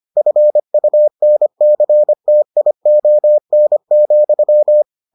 Morse code, heard as a keyed tone, about 25 wpm, 600 Hz. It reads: FUNCTION,